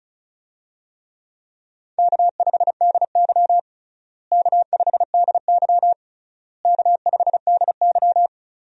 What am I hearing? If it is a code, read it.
Morse code, heard as a keyed tone, about 36 wpm, 700 Hz. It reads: K5DY K5DY K5DY
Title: USA Call Signs - 4 Characters - Repeated 3x - 35wpm